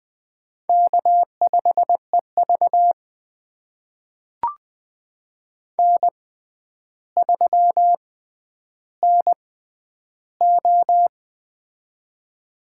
Morse code, heard as a keyed tone, about 20 wpm, spaced as for 10 wpm, 700 Hz. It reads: K5EV N3NO
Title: USA Call Signs - 4 Characters - 6x Character Spacing - 20wpm